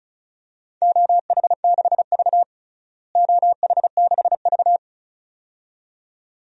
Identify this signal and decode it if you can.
Morse code, heard as a keyed tone, about 36 wpm, 700 Hz. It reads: OH6V OH6V
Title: DXCC-100 Call Signs - 4 Characters - Repeated 2x - 35wpm